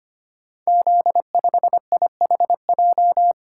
Morse code, heard as a keyed tone, about 25 wpm, 700 Hz. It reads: Z5IHJ
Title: Code Groups - 5 Letter:Numbers - 25wpm